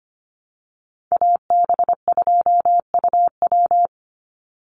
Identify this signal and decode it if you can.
Morse code, heard as a keyed tone, about 25 wpm, 700 Hz. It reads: AB2UW
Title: USA Call Signs 25wpm